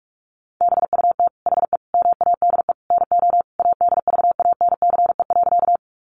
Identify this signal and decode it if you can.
Morse code, heard as a keyed tone, about 44 wpm, 700 Hz. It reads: BUT HE MADE NO ADVANCE.
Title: Sentences from Top 1400 Words 45wpm